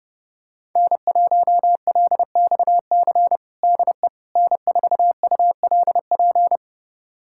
Morse code, heard as a keyed tone, about 32 wpm, 700 Hz. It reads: N1LXC DE N4ULP
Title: USA Callsign DE Callsign - 5 Characters - 30wpm